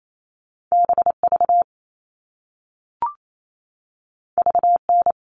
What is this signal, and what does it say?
Morse code, heard as a keyed tone, about 28 wpm, 700 Hz. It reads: BV VN
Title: All Cut Numbers - 2-Digit 28wpm